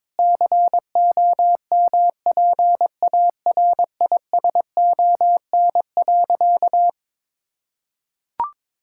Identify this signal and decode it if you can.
Morse code, heard as a keyed tone, about 21 wpm, 700 Hz. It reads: COMPARISON.
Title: Sentences from Top 1600 Words 22wpm